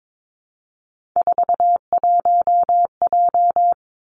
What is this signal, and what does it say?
Morse code, heard as a keyed tone, about 22 wpm, 700 Hz. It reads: 41J
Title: Three Challenging Letter-Number Combinations 22wpm